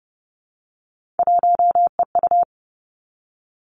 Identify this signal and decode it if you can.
Morse code, heard as a keyed tone, about 30 wpm, 700 Hz. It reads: 1EU
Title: Three Challenging Letter-Number Combinations 30wpm